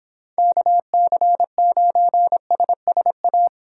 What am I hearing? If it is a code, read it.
Morse code, heard as a keyed tone, about 29 wpm, 700 Hz. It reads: KC9SSA